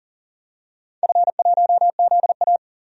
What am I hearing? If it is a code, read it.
Morse code, heard as a keyed tone, about 40 wpm, 700 Hz. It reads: F1ZA
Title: DXCC-100 Call Signs - 4 Characters - 40wpm